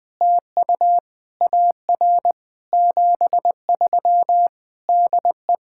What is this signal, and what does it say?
Morse code, heard as a keyed tone, about 20 wpm, 700 Hz. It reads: TU AR 73 DE
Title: POTA QSO Practice - 20wpm